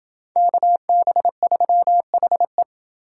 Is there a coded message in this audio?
Yes: Morse code from 0.4 to 2.6 s, heard as a keyed tone, about 29 wpm, 700 Hz. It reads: KB3HE